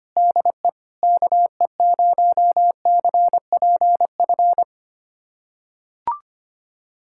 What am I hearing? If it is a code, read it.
Morse code, heard as a keyed tone, about 25 wpm, 700 Hz. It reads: DE KE0CPF
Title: USA Callsign DE Callsign - 6 Characters - 25wpm